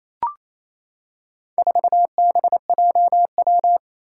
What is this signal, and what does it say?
Morse code, heard as a keyed tone, about 28 wpm, 700 Hz. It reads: 4BJW
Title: Four Challenging Letter-Number Combinations 28wpm